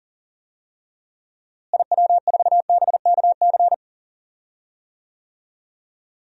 Morse code, heard as a keyed tone, about 40 wpm, 700 Hz. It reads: IW4BKC
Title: DXCC-100 Call Signs - 6 Characters - Repeated 2x - 40wpm